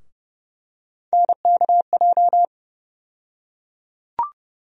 Morse code, heard as a keyed tone, about 30 wpm, 700 Hz. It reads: NKJ
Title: USA Call Sign Postfixes - 3 Letters 30wpm